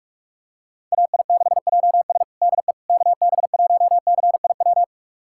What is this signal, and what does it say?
Morse code, heard as a keyed tone, about 41 wpm, 700 Hz. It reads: AI6JS DE KB1CIW
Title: USA Callsign DE Callsign 45wpm